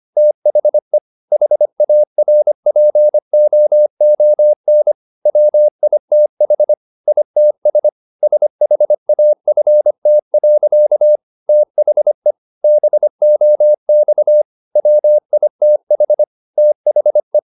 Morse code, heard as a keyed tone, about 25 wpm, 600 Hz. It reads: THE HARPOON WITH ITS SHAFT. THE BOX WITH THE
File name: the old man and the sea part 1 25wpm